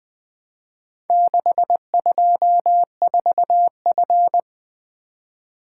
Morse code, heard as a keyed tone, about 20 wpm, 700 Hz. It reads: 624F
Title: Four Letter-Number Combinations - 20wpm